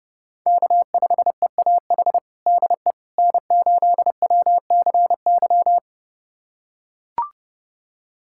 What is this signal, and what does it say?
Morse code, heard as a keyed tone, about 32 wpm, 700 Hz. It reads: K5EAH DE N8WCY
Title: USA Callsign DE Callsign - 5 Characters - 30wpm